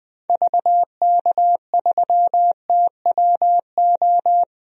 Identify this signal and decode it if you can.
Morse code, heard as a keyed tone, about 20 wpm, 700 Hz. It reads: VK3TWO
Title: DXCC-100 Call Signs - 6 Characters - 20wpm